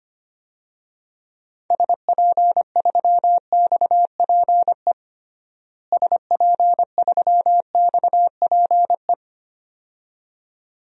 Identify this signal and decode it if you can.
Morse code, heard as a keyed tone, about 25 wpm, 700 Hz. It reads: SP3XPE SP3XPE
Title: DXCC-100 Call Signs - 6 Characters - Repeated 2x - 25wpm